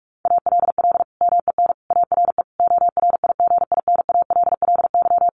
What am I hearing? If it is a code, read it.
Morse code, heard as a keyed tone, about 44 wpm, 700 Hz. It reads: ALL MEN ARE ORIGINALLY
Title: Sentences from Top 1500 Words 45wpm